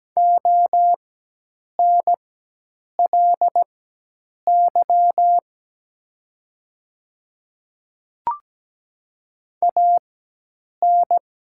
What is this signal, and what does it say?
Morse code, heard as a keyed tone, about 17 wpm, spaced as for 8 wpm, 700 Hz. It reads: ONLY AN
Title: 4x Character Spacing - Top 200 Words - 17wpm